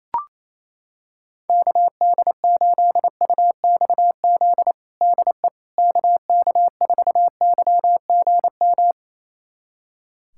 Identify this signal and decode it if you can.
Morse code, heard as a keyed tone, about 28 wpm, 700 Hz. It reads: KD8UXZ DE KK4YGM
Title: USA Callsign DE Callsign - 6 Characters - 28wpm